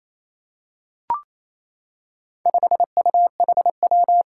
Morse code, heard as a keyed tone, about 27 wpm, 700 Hz. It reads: 5UHW
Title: Four Challenging Letter-Number Combinations 28wpm